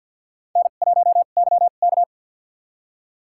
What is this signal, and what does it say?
Morse code, heard as a keyed tone, about 47 wpm, spaced as for 35 wpm, 700 Hz. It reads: N1YK
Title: USA Call Signs - 4 Characters - 2x Character Spacing - 50wpm